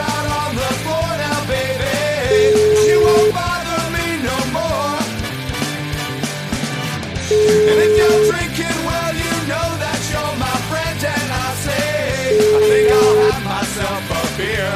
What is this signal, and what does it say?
Ringback tone (Europe, 425 Hz), with noise.